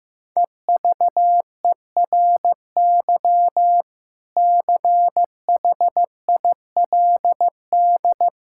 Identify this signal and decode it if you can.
Morse code, heard as a keyed tone, about 15 wpm, 700 Hz. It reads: EVERY CHILD
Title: Sentences from Top 1000 Words - Review Part 6 - 15wpm